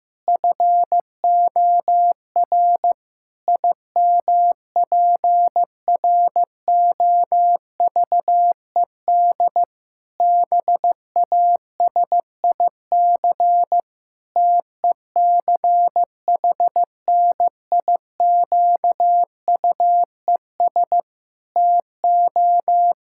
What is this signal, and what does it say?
Morse code, heard as a keyed tone, about 15 wpm, 700 Hz. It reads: FOR IMPROVED BASIC TECHNIQUES TO
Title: Pangram Challenge v2 - 15wpm